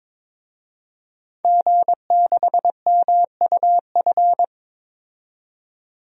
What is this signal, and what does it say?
Morse code, heard as a keyed tone, about 22 wpm, 700 Hz. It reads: G6MUF
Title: DXCC-100 Call Signs - 5 Characters - 22wpm